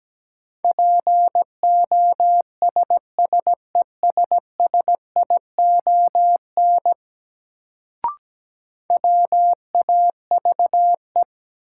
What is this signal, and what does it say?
Morse code, heard as a keyed tone, about 17 wpm, 700 Hz. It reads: POSSESSION WAVE
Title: Sentences from Top 1400 Words - New Word Drill Alt - 17wpm